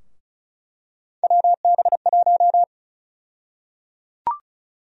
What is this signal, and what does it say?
Morse code, heard as a keyed tone, about 36 wpm, 700 Hz. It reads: WB1
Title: USA Call Sign Prefixes - 35wpm